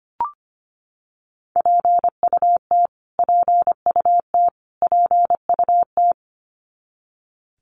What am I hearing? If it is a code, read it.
Morse code, heard as a keyed tone, about 25 wpm, 700 Hz. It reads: PUT PUT PUT
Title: Top 300 Words Repeated 25wpm